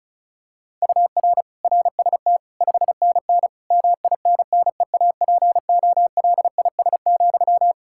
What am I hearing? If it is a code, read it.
Morse code, heard as a keyed tone, about 36 wpm, 700 Hz. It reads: UR RST 5NN MINNEAPOLIS,
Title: Practice QSO - Exchange 1 - Beginning - 35wpm